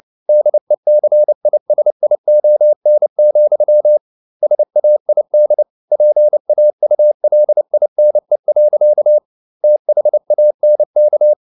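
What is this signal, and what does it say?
Morse code, heard as a keyed tone, about 29 wpm, 600 Hz. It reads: DECISION, SAID PAULINE. THANK
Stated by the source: Morse code at 29 wpm, sending Ken Follett's novel Never in English, from CW med Gnister.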